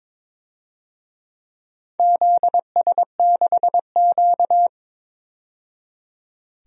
Morse code, heard as a keyed tone, about 22 wpm, 700 Hz. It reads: ZS6Q
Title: DXCC-100 Call Signs - 4 Characters - 22wpm